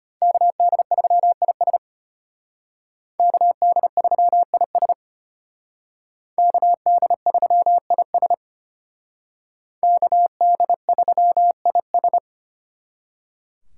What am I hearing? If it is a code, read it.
Morse code, heard as a keyed tone, about 33 wpm, 700 Hz. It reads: KD3IS KD3IS KD3IS KD3IS